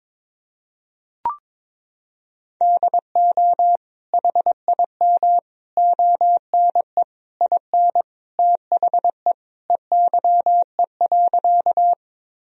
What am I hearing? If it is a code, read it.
Morse code, heard as a keyed tone, about 22 wpm, 700 Hz. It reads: DO HIM ONE IN THE EYE.
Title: Sentences from Top 1000 Words - Review Part 1 - 22wpm